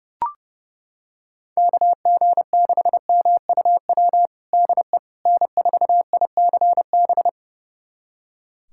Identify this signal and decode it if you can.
Morse code, heard as a keyed tone, about 30 wpm, 700 Hz. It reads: KG6MUW DE N4ICB
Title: USA Callsign DE Callsign 30wpm